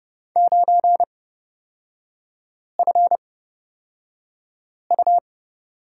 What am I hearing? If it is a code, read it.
Morse code, heard as a keyed tone, about 29 wpm, spaced as for 5 wpm, 690 Hz. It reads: 9FU
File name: Lesson 38e - Letter q - Three Characters 30wpm